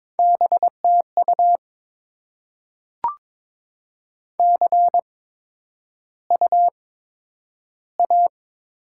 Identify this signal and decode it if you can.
Morse code, heard as a keyed tone, about 22 wpm, spaced as for 9 wpm, 700 Hz. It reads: BTU CUA
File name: Top 20 QSO Elements - 8x Character Spacing - 22wpm